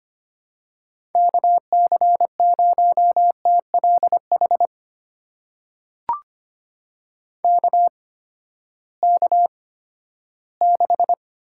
Morse code, heard as a keyed tone, about 25 wpm, spaced as for 14 wpm, 700 Hz. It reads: KC0TLH KK6
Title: USA Call Signs - 6 Characters - 8x Character Spacing - 25wpm